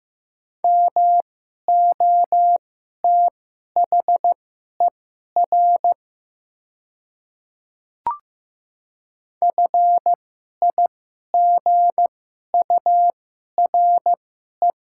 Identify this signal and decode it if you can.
Morse code, heard as a keyed tone, about 15 wpm, spaced as for 11 wpm, 690 Hz. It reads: MOTHER FIGURE
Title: 2x Character Spacing - Top 500 Words - 15wpm